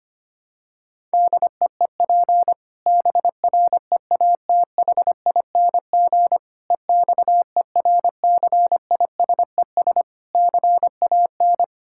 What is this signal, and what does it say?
Morse code, heard as a keyed tone, about 25 wpm, 700 Hz. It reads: DEEP BREATHING EXERCISES CAN
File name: practice / 25wpm / Difficult Sentences with Hints - 25wpm